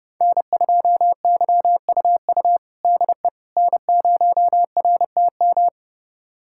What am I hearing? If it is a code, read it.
Morse code, heard as a keyed tone, about 30 wpm, 700 Hz. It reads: N2YUU DE N0RTM
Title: USA Callsign DE Callsign - 5 Characters - 30wpm